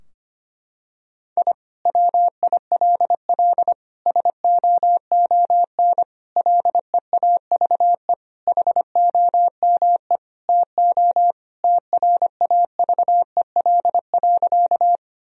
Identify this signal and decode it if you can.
Morse code, heard as a keyed tone, about 25 wpm, 700 Hz. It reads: I WILL SOON LEAVE HOME TO TRAVEL.
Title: Sentences from Top 1000 Words - Review Part 2 - 25wpm